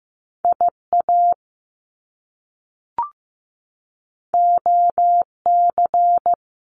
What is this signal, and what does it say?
Morse code, heard as a keyed tone, about 15 wpm, 700 Hz. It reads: IA OC
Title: Top Two Letter Combinations in Words 15wpm